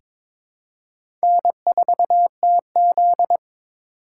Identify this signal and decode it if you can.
Morse code, heard as a keyed tone, about 22 wpm, 700 Hz. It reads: N4TZ
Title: K1USN SST QSO Practice - 22wpm